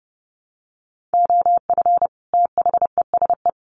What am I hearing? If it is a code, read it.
Morse code, heard as a keyed tone, about 30 wpm, 700 Hz. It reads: OF THESE
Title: Sets of 2 Words 30wpm